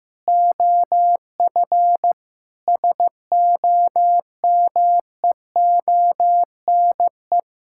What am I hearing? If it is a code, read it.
Morse code, heard as a keyed tone, about 15 wpm, 700 Hz. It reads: OF SOMEONE
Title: Sentences from Top 1000 Words - Review Part 5 - 15wpm